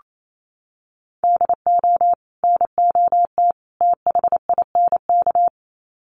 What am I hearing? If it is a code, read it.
Morse code, heard as a keyed tone, about 28 wpm, 700 Hz. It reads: DO NOT THINK
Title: Sets of 3 Words 28wpm